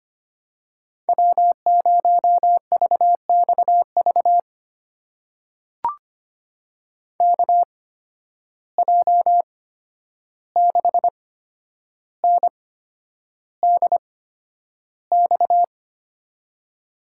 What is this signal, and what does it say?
Morse code, heard as a keyed tone, about 25 wpm, spaced as for 10 wpm, 700 Hz. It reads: W0VXV KJ6NDX
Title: USA Call Signs - 8x Character Spacing - 25wpm